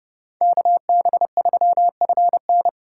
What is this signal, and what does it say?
Morse code, heard as a keyed tone, about 39 wpm, 700 Hz. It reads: KB3FN